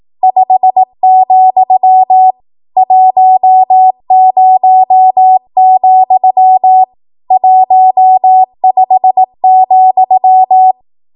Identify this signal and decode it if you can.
Morse code, heard as a keyed tone, about 18 wpm, 750 Hz. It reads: 5, 10, 15,
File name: Morse / 18WPM